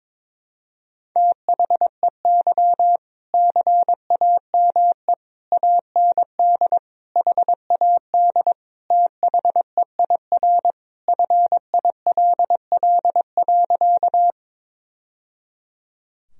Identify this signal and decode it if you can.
Morse code, heard as a keyed tone, about 22 wpm, 690 Hz. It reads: THEY CAME AND HAD THEIR FILL.
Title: Sentences from Top 1000 Words - Review Part 3 - 22wpm